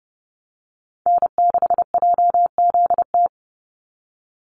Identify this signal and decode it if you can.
Morse code, heard as a keyed tone, about 30 wpm, 700 Hz. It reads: N6JZT
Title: NA SOTA Activator Callsigns 30wpm